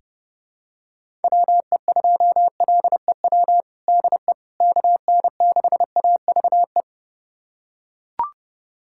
Morse code, heard as a keyed tone, about 30 wpm, 700 Hz. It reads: WE2LEW DE KN6AVE
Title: USA Callsign DE Callsign - 6 Characters - 30wpm